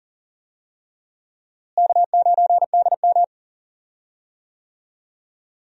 Morse code, heard as a keyed tone, about 40 wpm, 700 Hz. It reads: K9DM
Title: USA Call Signs - 4 Characters - Repeated 3x - 40wpm